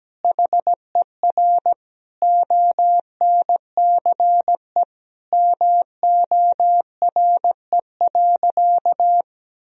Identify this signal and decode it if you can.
Morse code, heard as a keyed tone, about 17 wpm, 700 Hz. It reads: HER ONCE MORE.
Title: Sentences from Top 500 Words and Abbreviations 17wpm